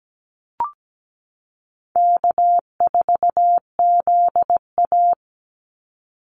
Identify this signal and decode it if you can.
Morse code, heard as a keyed tone, about 17 wpm, 700 Hz. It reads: K4ZA
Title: Easy Call Signs - 17wpm